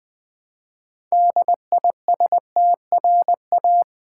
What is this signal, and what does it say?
Morse code, heard as a keyed tone, about 20 wpm, 700 Hz. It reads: DISTRA
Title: Build A Word - 3 Letters At A Time - 8-Letter Words - 20wpm